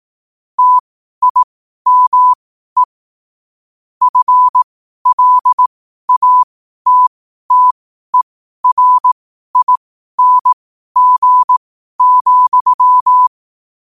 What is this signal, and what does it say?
Morse code, heard as a keyed tone, about 18 wpm, spaced as for 12 wpm, 1000 Hz. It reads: TIME FLATTERING,
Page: 12-wpm-morse-code-quotes